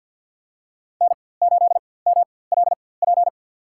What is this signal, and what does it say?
Morse code, heard as a keyed tone, about 52 wpm, spaced as for 23 wpm, 700 Hz. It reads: N8MLP
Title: USA Call Signs - 5 Characters - 4x Character Spacing - 50wpm